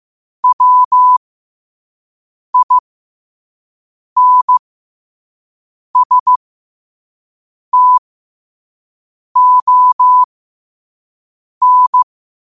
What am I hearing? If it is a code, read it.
Morse code, heard as a keyed tone, about 15 wpm, spaced as for 5 wpm, 1000 Hz. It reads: WINSTON
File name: qotdfiles / 5wpm